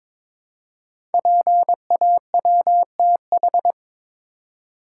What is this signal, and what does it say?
Morse code, heard as a keyed tone, about 23 wpm, 700 Hz. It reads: PAWTH
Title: Code Groups - 5 Letters - 22wpm